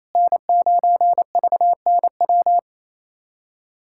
Morse code, heard as a keyed tone, about 28 wpm, 700 Hz. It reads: N9VNW
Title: USA Callsign DE Callsign - 5 Characters - 28wpm